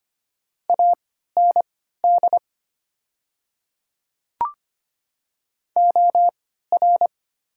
Morse code, heard as a keyed tone, about 25 wpm, spaced as for 14 wpm, 700 Hz. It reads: AND OR
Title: 3x Character Spacing - Top 100 Words - 25wpm